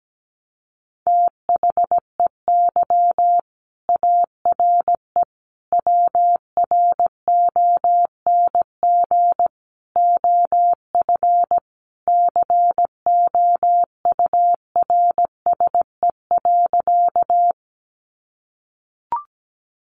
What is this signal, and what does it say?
Morse code, heard as a keyed tone, about 17 wpm, 700 Hz. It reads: THEY ARE WRONG OF COURSE.
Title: Sentences from Top 900 Words 17wpm